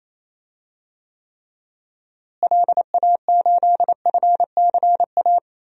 Morse code, heard as a keyed tone, about 27 wpm, 700 Hz. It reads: LA8FCA
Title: DXCC-100 Call Signs - 6 Characters - Repeated 2x - 28wpm